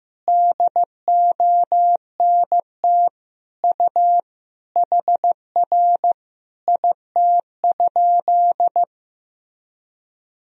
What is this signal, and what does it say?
Morse code, heard as a keyed tone, about 15 wpm, 700 Hz. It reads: DONT U HR IT?
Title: Sentences from Top 300 Words and Abbreviations 15wpm